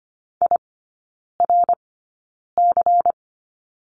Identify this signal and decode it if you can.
Morse code, heard as a keyed tone, about 25 wpm, spaced as for 9 wpm, 700 Hz. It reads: IRC
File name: course / 25:10wpm / 100 Lesson 17e - Letter u - Three Characters 25wpm